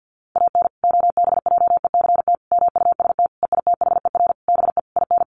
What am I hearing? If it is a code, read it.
Morse code, heard as a keyed tone, about 50 wpm, 700 Hz. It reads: AN OBJECT MUST EITHER BE IN
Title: Sentences from Top 1400 Words 50wpm